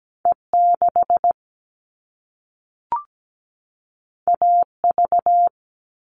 Morse code, heard as a keyed tone, about 17 wpm, 700 Hz. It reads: E6 AV
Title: All Cut Numbers - 2-Digit 17wpm